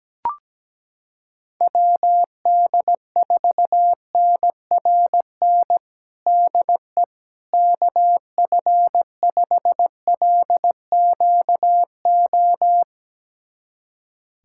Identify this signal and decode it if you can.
Morse code, heard as a keyed tone, about 17 wpm, 700 Hz. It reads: WD4NRN DE KF5LQO
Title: USA Callsign DE Callsign - 6 Characters - 17wpm